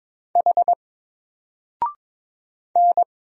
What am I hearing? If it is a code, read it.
Morse code, heard as a keyed tone, about 22 wpm, 700 Hz. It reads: H N
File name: Single-Letter-Number-22wpm